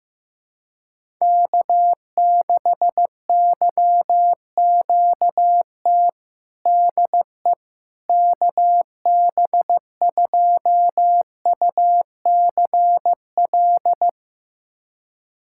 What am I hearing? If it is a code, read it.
Morse code, heard as a keyed tone, about 16 wpm, 700 Hz. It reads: K6YQT DE KB2UCL
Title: USA Callsign DE Callsign 15wpm